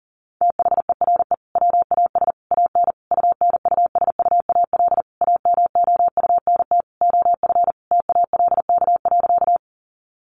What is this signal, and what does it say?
Morse code, heard as a keyed tone, about 40 wpm, 700 Hz. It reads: THERE WAS AN UNUSUAL AMOUNT OF TALK.
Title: Sentences from Top 1400 Words 40wpm